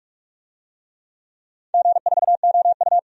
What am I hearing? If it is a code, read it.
Morse code, heard as a keyed tone, about 46 wpm, 700 Hz. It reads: G4OU
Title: DXCC-100 Call Signs - 4 Characters - 45wpm